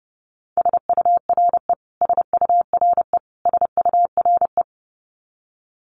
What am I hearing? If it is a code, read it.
Morse code, heard as a keyed tone, about 30 wpm, 700 Hz. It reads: SURE SURE SURE
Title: Top 300 Words Repeated 30wpm